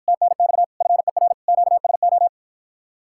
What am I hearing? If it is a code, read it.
Morse code, heard as a keyed tone, about 52 wpm, 700 Hz. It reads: TNX FER QSO